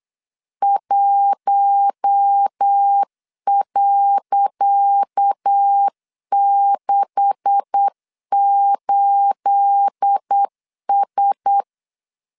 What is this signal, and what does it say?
Morse code, heard as a keyed tone, about 8 wpm, 800 Hz. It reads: 1.68S
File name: LEZIONI / 10 SIGNS / 10C DOT 8wpm 3649